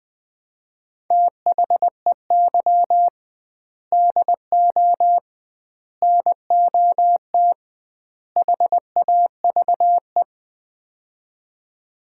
Morse code, heard as a keyed tone, about 20 wpm, 700 Hz. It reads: THEY DO NOT HAVE
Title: 2x Word-Spacing - Sets of 4 Words - 20wpm